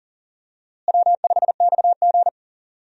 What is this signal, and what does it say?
Morse code, heard as a keyed tone, about 39 wpm, 700 Hz. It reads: W5XG
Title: USA Call Signs - 4 Characters - Repeated 2x - 40wpm